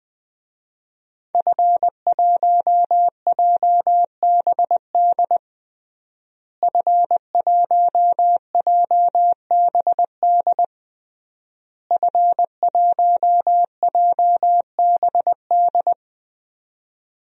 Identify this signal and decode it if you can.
Morse code, heard as a keyed tone, about 20 wpm, 700 Hz. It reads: F1JBD F1JBD F1JBD
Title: DXCC-100 Call Signs - 5 Characters - Repeated 3x - 20wpm